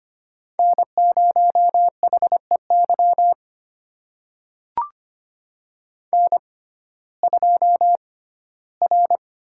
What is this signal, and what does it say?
Morse code, heard as a keyed tone, about 25 wpm, spaced as for 15 wpm, 700 Hz. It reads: N0HEY N2R
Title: USA Call Signs - 5 Characters - 6x Character Spacing - 25wpm